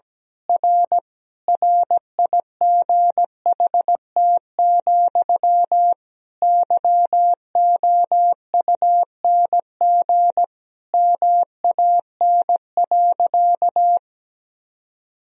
Morse code, heard as a keyed tone, about 17 wpm, 700 Hz. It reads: R RIGHT, YOUNG MAN.
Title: Sentences from Top 400 Words and Abbreviations 17wpm